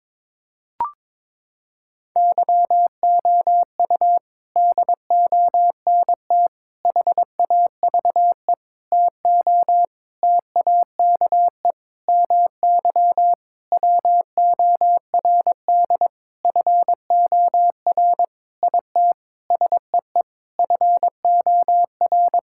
Morse code, heard as a keyed tone, about 22 wpm, 700 Hz. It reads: YOU DONT HAVE TO TAKE MY WORD FOR IT SEE FOR
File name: practice / 22wpm / Sentences with 8-Word Phrases - 22wpm